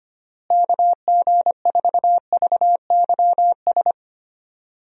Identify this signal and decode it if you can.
Morse code, heard as a keyed tone, about 25 wpm, 700 Hz. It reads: KG4VYS
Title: USA Call Signs - 6 Characters - Repeated 2x - 25wpm